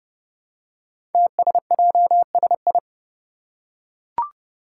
Morse code, heard as a keyed tone, about 30 wpm, 700 Hz. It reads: TSJSI
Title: Five Challenging Letter-Number Combinations 30wpm